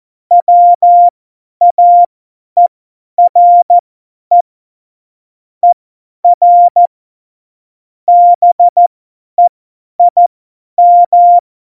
Morse code, heard as a keyed tone, about 14 wpm, spaced as for 10 wpm, 700 Hz. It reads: WAERE ER BEIM
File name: Morsekurs / Witze / Witz 01. September 2020, 50 F70